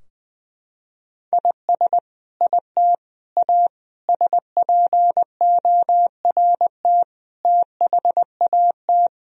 Morse code, heard as a keyed tone, about 20 wpm, 700 Hz. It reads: IS IT A SPORT THAT
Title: Sentences from Top 1000 Words - Review Part 4 - 20wpm